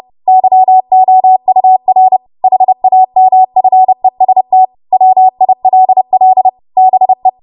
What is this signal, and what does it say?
Morse code, heard as a keyed tone, about 30 wpm, 750 Hz. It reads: YOUR HAMFEST WILL BE
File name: Morse / 30WPM